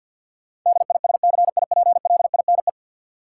Morse code, heard as a keyed tone, about 51 wpm, 700 Hz. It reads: DISCIPLINE
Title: Sentences from Top 900 Words - New Word Drill - 50wpm